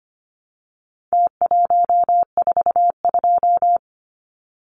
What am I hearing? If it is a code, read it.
Morse code, heard as a keyed tone, about 25 wpm, 700 Hz. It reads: T142
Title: Numbers - Contest Style - 4-Digit 25wpm